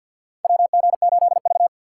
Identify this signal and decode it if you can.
Morse code, heard as a keyed tone, about 53 wpm, 700 Hz. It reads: WG8V
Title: USA Call Signs - 4x Character Spacing - 50wpm